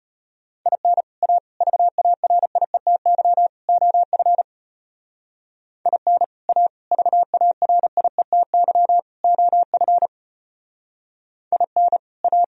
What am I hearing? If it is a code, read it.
Morse code, heard as a keyed tone, about 32 wpm, 700 Hz. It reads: IN A VARIETY OF IN A VARIETY OF IN A